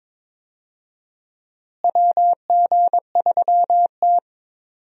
Morse code, heard as a keyed tone, about 21 wpm, 700 Hz. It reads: WG3T
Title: USA Call Signs - 4 Characters - Repeated 3x - 22wpm